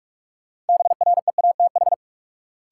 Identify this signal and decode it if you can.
Morse code, heard as a keyed tone, about 46 wpm, 700 Hz. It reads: BREATH
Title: Sets of 5 Words Encore - New Word Drill - 45wpm